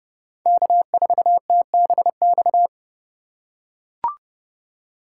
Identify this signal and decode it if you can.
Morse code, heard as a keyed tone, about 30 wpm, 700 Hz. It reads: K4TBX
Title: USA Call Signs - 5 Characters - 4x Character Spacing - 30wpm